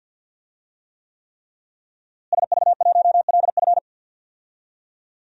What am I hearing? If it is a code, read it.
Morse code, heard as a keyed tone, about 46 wpm, 700 Hz. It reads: SV1LF
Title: DXCC-100 Call Signs - 5 Characters - Repeated 3x - 50wpm